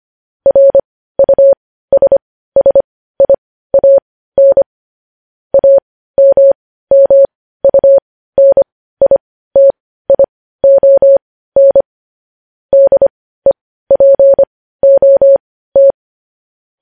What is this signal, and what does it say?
Morse code, heard as a keyed tone, about 25 wpm, spaced as for 15 wpm, 550 Hz. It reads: RUSSIAN AMMUNITION DEPOT
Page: News headlines in morse code at 15 wpm